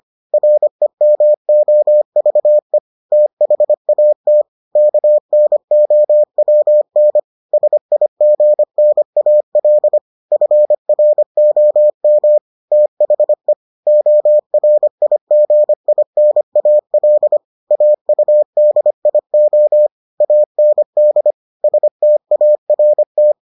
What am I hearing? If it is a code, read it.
Morse code, heard as a keyed tone, about 25 wpm, 600 Hz. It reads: REMOVE THAT KNOWN SIGNAL FROM THE ORIGINAL AUDIO AND START